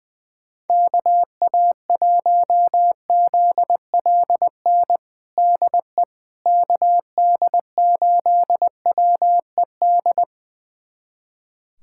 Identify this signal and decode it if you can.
Morse code, heard as a keyed tone, about 21 wpm, 700 Hz. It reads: KA1ZLN DE KD8WED
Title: USA Callsign DE Callsign - 6 Characters - 20wpm